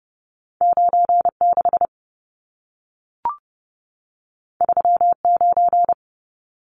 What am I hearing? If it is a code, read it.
Morse code, heard as a keyed tone, about 30 wpm, 700 Hz. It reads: 96 39
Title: Numbers - 2-Digit 30wpm